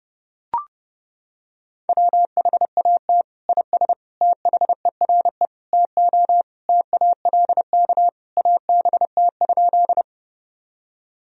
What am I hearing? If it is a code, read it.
Morse code, heard as a keyed tone, about 30 wpm, 700 Hz. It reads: WHAT IS THERE TO TALK ABT?
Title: Sentences from Top 400 Words and Abbreviations 30wpm